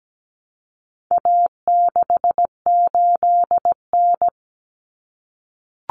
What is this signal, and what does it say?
Morse code, heard as a keyed tone, about 17 wpm, 700 Hz. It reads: A68N
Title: Common Cut Numbers - 4-Digit 17wpm